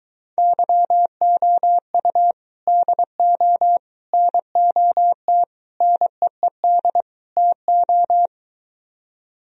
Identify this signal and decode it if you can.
Morse code, heard as a keyed tone, about 23 wpm, 700 Hz. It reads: YOU DO NOT NEED TO